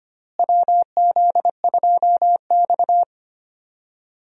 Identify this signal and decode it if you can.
Morse code, heard as a keyed tone, about 25 wpm, 700 Hz. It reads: WZ2X
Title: USA Call Signs - 4 Characters - Repeated 2x - 25wpm